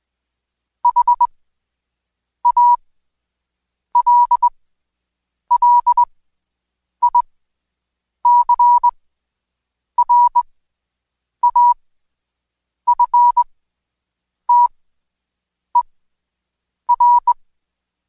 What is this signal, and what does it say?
Morse code, heard as a keyed tone, about 21 wpm, spaced as for 7 wpm, 970 Hz. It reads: HALLICRAFTER
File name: Code Quick / AudioDownld2 / Seg113-Day18-DayDrill 18